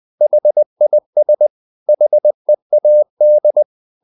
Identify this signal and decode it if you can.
Morse code, heard as a keyed tone, about 20 wpm, 600 Hz. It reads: HIS HEAD
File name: the old man and the sea part 2 20wpm